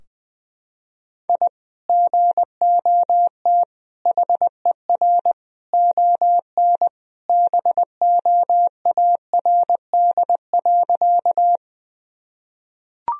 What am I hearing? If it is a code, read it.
Morse code, heard as a keyed tone, about 20 wpm, 700 Hz. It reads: I GOT HER ON BOARD.
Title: Sentences from Top 1000 Words - Review Part 4 - 20wpm